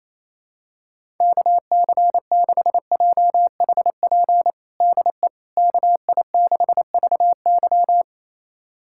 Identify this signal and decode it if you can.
Morse code, heard as a keyed tone, about 28 wpm, 700 Hz. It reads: KC6JHP DE KI6VY
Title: USA Callsign DE Callsign 28wpm